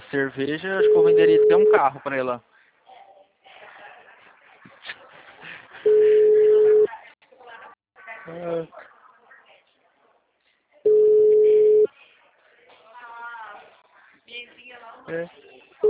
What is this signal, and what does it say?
Ringback tone (Europe, 425 Hz), with speech.
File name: out- -2016- - - .62892